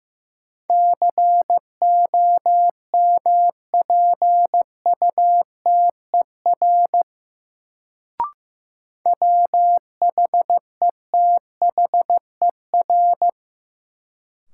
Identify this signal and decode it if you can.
Morse code, heard as a keyed tone, about 16 wpm, 700 Hz. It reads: COMPUTER WHETHER
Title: Sentences from Top 1000 Words - Review Part 3 - New Word Drill - 15wpm